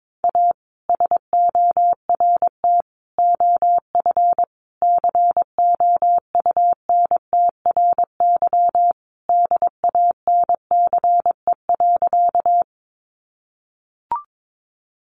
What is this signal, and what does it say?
Morse code, heard as a keyed tone, about 22 wpm, 700 Hz. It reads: A SORT OF COUNTRY DANCE.
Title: Sentences from Top 1300 Words 22wpm